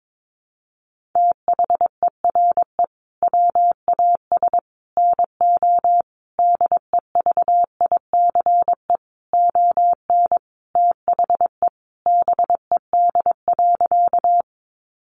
Morse code, heard as a keyed tone, about 22 wpm, 700 Hz. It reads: THERE WAS NO DEVICE ON THE BED.
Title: Sentences from Top 1500 Words 22wpm